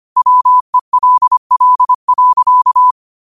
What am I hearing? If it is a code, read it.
Morse code, heard as a keyed tone, about 25 wpm, 1000 Hz. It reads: WELL.
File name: qotdfiles / 25wpm